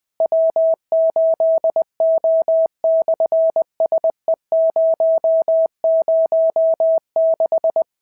Morse code, heard as a keyed tone, about 20 wpm, 650 Hz. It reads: W8O/SE006
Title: SOTA QSO Practice 20wpm